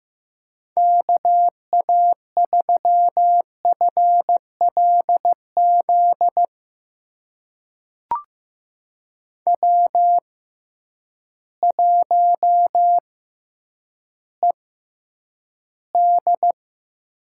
Morse code, heard as a keyed tone, about 15 wpm, spaced as for 9 wpm, 700 Hz. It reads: KA3FLZ W1ED
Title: USA Call Signs - 6x Character Spacing - 15wpm